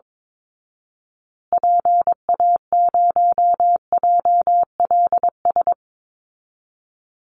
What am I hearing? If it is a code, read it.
Morse code, heard as a keyed tone, about 22 wpm, 700 Hz. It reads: PA0JLS
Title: Call Signs Repeated 2x - 22wpm